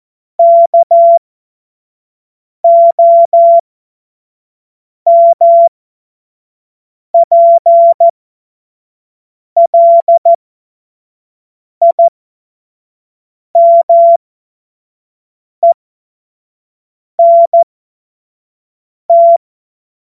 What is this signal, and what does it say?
Morse code, heard as a keyed tone, about 14 wpm, spaced as for 5 wpm, 680 Hz. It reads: KOMPLIMENT
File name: Morsekurs / Witze / Witz 12. April 2022 25 F70